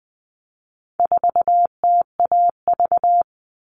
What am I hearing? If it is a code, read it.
Morse code, heard as a keyed tone, about 20 wpm, 700 Hz. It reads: 4TAV
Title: All Cut Numbers - 4-Digit 20wpm